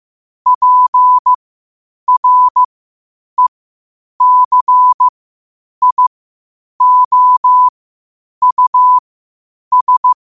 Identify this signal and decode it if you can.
Morse code, heard as a keyed tone, about 15 wpm, spaced as for 8 wpm, 1000 Hz. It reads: PRECIOUS
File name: qotdfiles / 8wpm